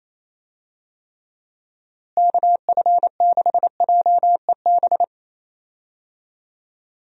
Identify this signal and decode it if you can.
Morse code, heard as a keyed tone, about 28 wpm, 700 Hz. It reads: KF6JEB
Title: USA Call Signs - 6 Characters - Repeated 2x - 28wpm